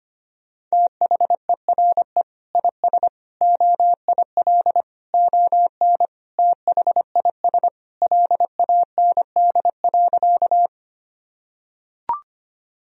Morse code, heard as a keyed tone, about 25 wpm, 700 Hz. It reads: THERE IS OIL ON THIS LAND.
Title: Sentences from Top 1000 Words - Review Part 4 - 25wpm